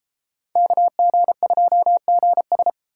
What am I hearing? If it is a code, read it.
Morse code, heard as a keyed tone, about 31 wpm, 700 Hz. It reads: KG2GS